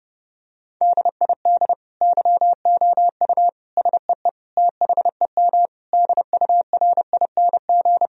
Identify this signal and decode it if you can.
Morse code, heard as a keyed tone, about 30 wpm, 700 Hz. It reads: DID YOU SEE THEM DURING
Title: Sentences from Top 1000 Words - Review Part 2 - 30wpm